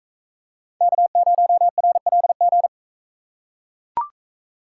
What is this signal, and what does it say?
Morse code, heard as a keyed tone, about 41 wpm, 700 Hz. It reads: K0RLG